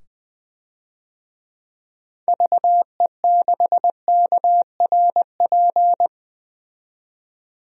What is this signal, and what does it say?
Morse code, heard as a keyed tone, about 20 wpm, 700 Hz. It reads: VE6KRP
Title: DXCC-100 Call Signs - 6 Characters - 20wpm